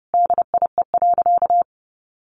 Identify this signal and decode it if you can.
Morse code, heard as a keyed tone, about 30 wpm, 700 Hz. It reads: DIE.
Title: Sentences from Top 1300 Words 30wpm